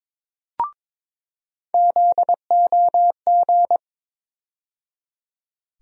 Morse code, heard as a keyed tone, about 22 wpm, 700 Hz. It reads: ZOG
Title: USA Call Sign Postfixes - 3 Letters 22wpm